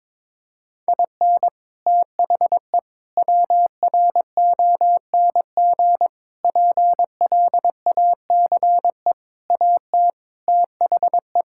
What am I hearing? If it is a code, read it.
Morse code, heard as a keyed tone, about 22 wpm, 700 Hz. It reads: IN THE WRONG PLACE AT THE
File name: practice / 22wpm / Sets of 8 Words - 22wpm